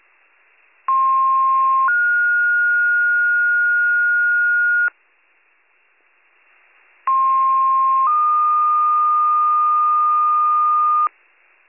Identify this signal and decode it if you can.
Two-tone page, heard as a one-second tone, then a three-second tone, about 1060 then 1430 Hz, about 1060 then 1220 Hz, with noise. It is heard from 0.9 to 11.1 s.